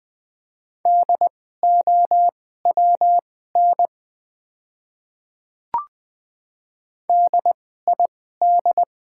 Morse code, heard as a keyed tone, about 20 wpm, spaced as for 15 wpm, 690 Hz. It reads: DOWN DID
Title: 2x Character Spacing - Top 200 Words - 20wpm